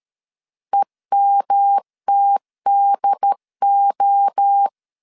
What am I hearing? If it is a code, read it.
Morse code, heard as a keyed tone, about 13 wpm, 800 Hz. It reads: EMTDO
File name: CW / DET / 1 3 1500C S C 12wpm 2750